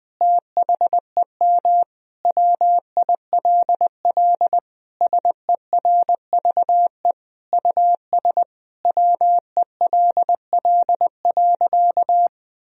Morse code, heard as a keyed tone, about 20 wpm, 700 Hz. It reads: THEM WILL SERVE US WELL.
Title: Sentences from Top 1000 Words - Review Part 4 - 20wpm